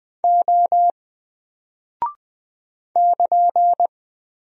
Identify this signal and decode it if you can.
Morse code, heard as a keyed tone, about 20 wpm, 700 Hz. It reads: O <KN>
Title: Single Character (Letter:Number:Punctuation:Prosign) Alt 20wpm